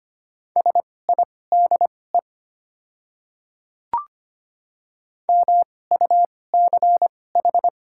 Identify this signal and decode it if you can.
Morse code, heard as a keyed tone, about 25 wpm, spaced as for 18 wpm, 690 Hz. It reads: SIDE MUCH
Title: 2x Character Spacing - Top 200 Words - 25wpm